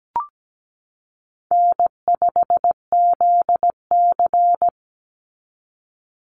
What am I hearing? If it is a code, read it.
Morse code, heard as a keyed tone, about 17 wpm, 700 Hz. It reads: N5ZC
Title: NA SOTA Chaser Callsigns 17wpm